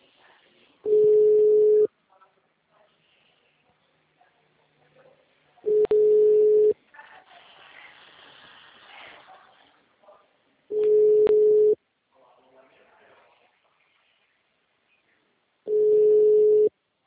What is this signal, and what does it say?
Ringback tone (Europe, 425 Hz).
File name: out- -2017- - - .426